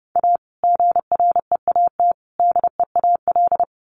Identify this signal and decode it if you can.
Morse code, heard as a keyed tone, about 30 wpm, 700 Hz. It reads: A GREAT DEAL
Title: Sets of 3 Words 30wpm